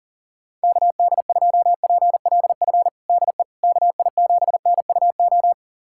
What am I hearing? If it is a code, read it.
Morse code, heard as a keyed tone, about 40 wpm, 700 Hz. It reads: KD2PLF DE KI7NUO
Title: USA Callsign DE Callsign - 6 Characters - 40wpm